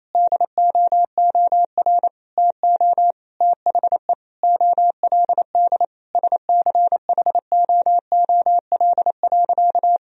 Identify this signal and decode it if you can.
Morse code, heard as a keyed tone, about 28 wpm, 700 Hz. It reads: DOOR TO THE OLD SCHOOL.
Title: Sentences from Top 1000 Words - Review Part 1 - 28wpm